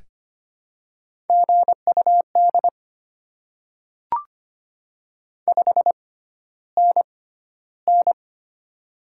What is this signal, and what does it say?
Morse code, heard as a keyed tone, about 25 wpm, spaced as for 13 wpm, 700 Hz. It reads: GUD 5NN
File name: Top 20 QSO Elements - 6x Character Spacing - 25wpm